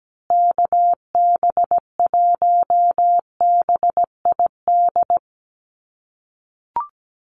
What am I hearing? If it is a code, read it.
Morse code, heard as a keyed tone, about 17 wpm, 700 Hz. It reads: KB1BID
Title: USA Call Signs Repeated 2x 17wpm